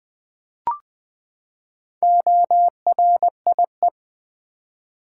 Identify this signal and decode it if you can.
Morse code, heard as a keyed tone, about 20 wpm, 700 Hz. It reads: ORIE
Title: Build A Word - 4 Letters At A Time - 11-Letter Words - 20wpm